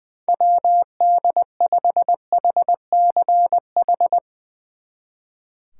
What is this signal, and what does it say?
Morse code, heard as a keyed tone, about 21 wpm, 700 Hz. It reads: WD5HCH
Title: Speed Racing - USA 2x3 Call Signs 20wpm